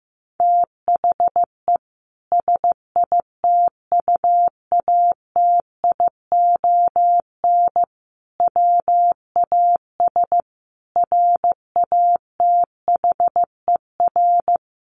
Morse code, heard as a keyed tone, about 14 wpm, 700 Hz. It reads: THE SITUATION WAS RATHER
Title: Sentences from Top 1300 Words 15wpm